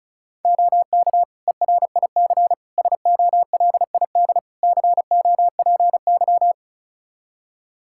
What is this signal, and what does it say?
Morse code, heard as a keyed tone, about 34 wpm, 700 Hz. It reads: OK ERIC SOLID COPY
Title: Practice QSO - Exchange 3 - More Info - 35wpm